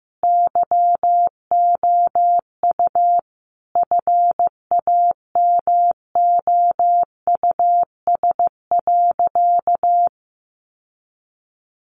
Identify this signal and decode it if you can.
Morse code, heard as a keyed tone, about 14 wpm, 700 Hz. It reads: YOU FAMOUS.
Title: Sentences from Top 1500 Words 15wpm